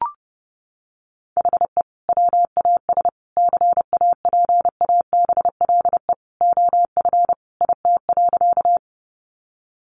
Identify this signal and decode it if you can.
Morse code, heard as a keyed tone, about 28 wpm, 700 Hz. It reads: HE WAS CAPABLE OF IT.
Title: Sentences from Top 1200 Words 30wpm